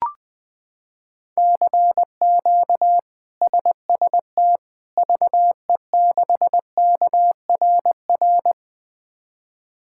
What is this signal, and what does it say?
Morse code, heard as a keyed tone, about 20 wpm, 700 Hz. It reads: CQ SST VE6KRR
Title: K1USN SST QSO Practice - 20wpm